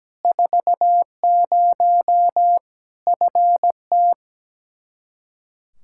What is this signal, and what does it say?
Morse code, heard as a keyed tone, about 17 wpm, 700 Hz. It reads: 40 FT
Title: K1USN SST QSO Practice 17wpm